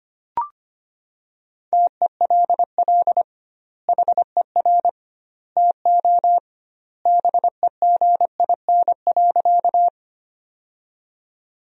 Morse code, heard as a keyed tone, about 25 wpm, 700 Hz. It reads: TELL HER TO BEGIN.
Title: 2x Word-Spacing - Sentences from Top 300 Words - 25wpm